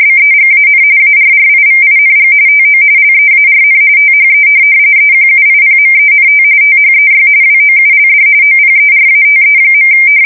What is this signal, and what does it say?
RTTY, about 45.45 baud. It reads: BEGAN ARRIVING TO LEARN THE EQUIPMENT THEY WOULD OPERATE.
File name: fsk170 / USS-New-Jersey